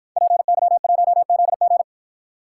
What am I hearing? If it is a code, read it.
Morse code, heard as a keyed tone, about 53 wpm, 700 Hz. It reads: PY1ZG